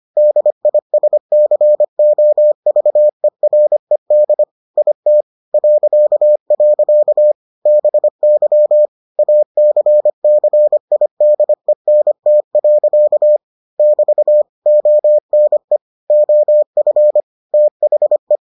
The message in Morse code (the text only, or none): DISCOVERED IT .. BY ACCIDENT. <BT> ONE OF THE